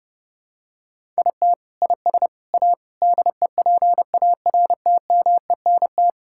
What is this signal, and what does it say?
Morse code, heard as a keyed tone, about 30 wpm, 700 Hz. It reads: IT IS A DEPARTMENT
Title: Sentences from Top 1000 Words - Review Part 5 - 30wpm